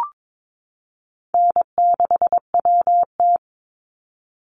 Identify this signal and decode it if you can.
Morse code, heard as a keyed tone, about 22 wpm, 700 Hz. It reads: N6WT
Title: NA SOTA Chaser Callsigns 22wpm